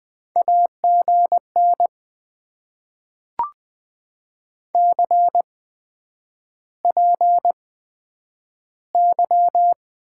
Morse code, heard as a keyed tone, about 20 wpm, spaced as for 9 wpm, 700 Hz. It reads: AGN CPY
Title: Top 90 QSO Elements - 8x Character Spacing - 20wpm